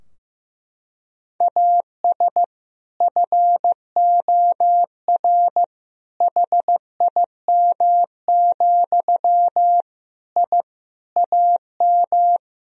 Morse code, heard as a keyed tone, about 15 wpm, 700 Hz. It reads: AS FOR HIM, I AM
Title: Sentences from Top 1000 Words - Review Part 2 - 15wpm